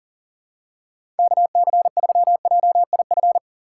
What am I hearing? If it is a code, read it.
Morse code, heard as a keyed tone, about 38 wpm, 700 Hz. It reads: KC3JIF
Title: USA Call Signs - 6 Characters - 4x Character Spacing - 40wpm